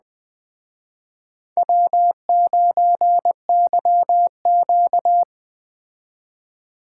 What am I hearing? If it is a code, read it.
Morse code, heard as a keyed tone, about 20 wpm, 700 Hz. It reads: W9YQ
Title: Speed Racing - USA 1x2 Call Signs 17wpm